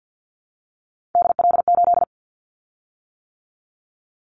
Morse code, heard as a keyed tone, about 50 wpm, 700 Hz. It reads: DL8
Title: Callsign Prefixes for Top 50 DXCC - 50wpm